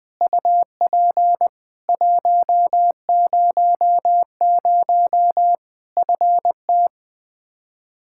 Morse code, heard as a keyed tone, about 21 wpm, 700 Hz. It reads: UP 100 FT
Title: Practice QSO - Exchange 2 - Info - 20wpm